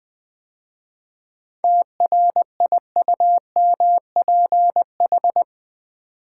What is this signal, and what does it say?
Morse code, heard as a keyed tone, about 20 wpm, 700 Hz. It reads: TRIUMPH
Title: 7 Letter Words - First 3 Letters Given - 20wpm